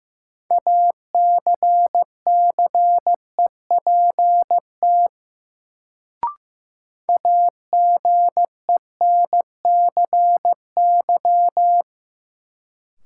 Morse code, heard as a keyed tone, about 15 wpm, 700 Hz. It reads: ACCEPT AGENCY
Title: Sentences from Top 1300 Words - New Word Drill Alt - 15wpm